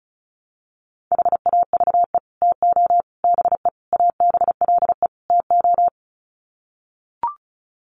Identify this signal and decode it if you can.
Morse code, heard as a keyed tone, about 34 wpm, 700 Hz. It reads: HAVE TO BE ABLE TO
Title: Sets of 5 Words 35wpm